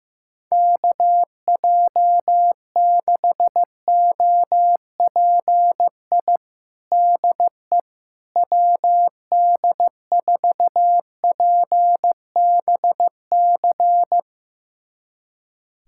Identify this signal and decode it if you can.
Morse code, heard as a keyed tone, about 15 wpm, 700 Hz. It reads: KJ6OPI DE WD4PBC
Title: USA Callsign DE Callsign - 6 Characters - 15wpm